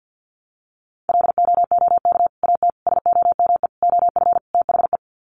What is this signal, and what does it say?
Morse code, heard as a keyed tone, about 50 wpm, 700 Hz. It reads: LOOK AT SOME OF THE
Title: Sets of 5 Words Encore 50wpm